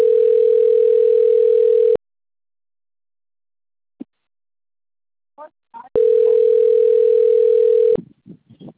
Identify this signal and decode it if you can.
Ringback tone (North America), 440 + 480 Hz.